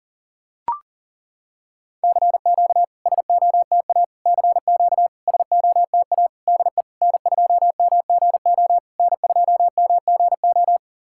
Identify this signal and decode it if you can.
Morse code, heard as a keyed tone, about 40 wpm, 700 Hz. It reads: CQ SOTA CQ SOTA DE N2MGO N2MGO
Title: SOTA QSO Practice 40wpm